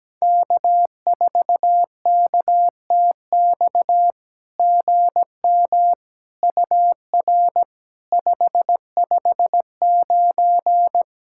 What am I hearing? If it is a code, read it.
Morse code, heard as a keyed tone, about 17 wpm, 700 Hz. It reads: K4KTX GM UR 559
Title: SOTA QSO Practice 17wpm